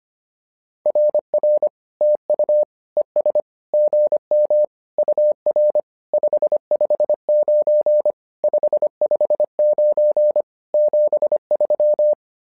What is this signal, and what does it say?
Morse code, heard as a keyed tone, about 25 wpm, 600 Hz. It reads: RR TU ES GM UR 559 559 73
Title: SOTA QSO Practice 25wpm